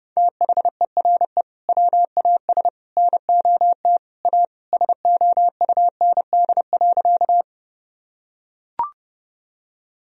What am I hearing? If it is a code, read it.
Morse code, heard as a keyed tone, about 32 wpm, 700 Hz. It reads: THERE WAS NOT A SOUND.
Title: Sentences from Top 1000 Words - Review Part 1 - 30wpm